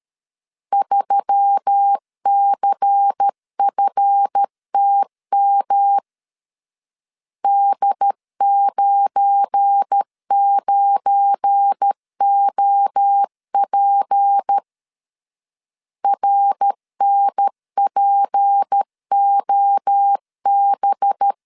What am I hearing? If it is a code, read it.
Morse code, heard as a keyed tone, about 13 wpm, 800 Hz. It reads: 3CFTM D99OP RNPOB
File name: LEZIONI / 7 GF9Z / 7 GF9 12wpm 2319